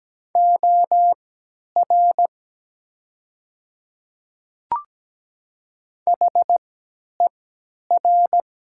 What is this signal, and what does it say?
Morse code, heard as a keyed tone, about 17 wpm, spaced as for 10 wpm, 700 Hz. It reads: OR HER
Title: 3x Character Spacing - Top 100 Words - 17wpm